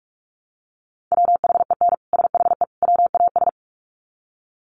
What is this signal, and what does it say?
Morse code, heard as a keyed tone, about 44 wpm, 700 Hz. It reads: WHEN SHE WAS
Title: Sets of 3 Words 45wpm